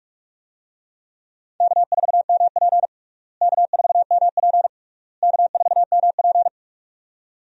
Morse code, heard as a keyed tone, about 48 wpm, 700 Hz. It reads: K4MP K4MP K4MP
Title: USA Call Signs - 4 Characters - Repeated 3x - 45wpm